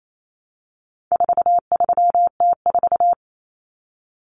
Morse code, heard as a keyed tone, about 28 wpm, 700 Hz. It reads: 43T4
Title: Common Cut Numbers - 4-Digit 28wpm